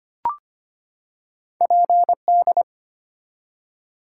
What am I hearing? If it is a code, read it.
Morse code, heard as a keyed tone, about 25 wpm, 700 Hz. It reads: PD
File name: Two Letter-Number Combinations - 25wpm